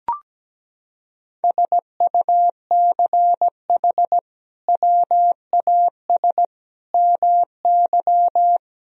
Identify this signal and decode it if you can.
Morse code, heard as a keyed tone, about 17 wpm, 700 Hz. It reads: SUCH WAS MY
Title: Sentences from Top 1000 Words - Review Part 1 - 17wpm